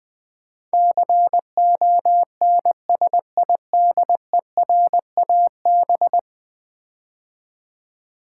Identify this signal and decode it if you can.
Morse code, heard as a keyed tone, about 20 wpm, 700 Hz. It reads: CONSIDERAB
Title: Build A Word - 5 Letters At A Time - 12-Letter Words - 20wpm